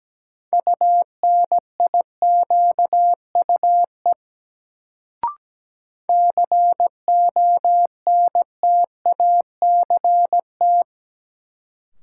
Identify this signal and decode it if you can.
Morse code, heard as a keyed tone, about 17 wpm, 700 Hz. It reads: UNIQUE CONTACT
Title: Sentences from Top 1000 Words - Review Part 7 - New Word Drill - 17wpm